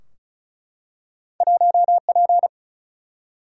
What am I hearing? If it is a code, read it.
Morse code, heard as a keyed tone, about 36 wpm, 700 Hz. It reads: 1P
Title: Single-Letter Single-Number 35wpm